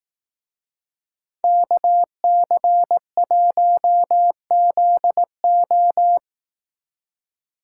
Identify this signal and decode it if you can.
Morse code, heard as a keyed tone, about 17 wpm, 700 Hz. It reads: KC1ZO
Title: Speed Racing - USA 2x2 Call Signs 15wpm